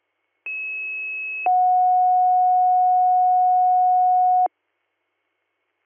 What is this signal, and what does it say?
Two-tone page, heard as a one-second tone, then a three-second tone, about 2600 then 730 Hz. It is heard from 0.5 to 4.5 s.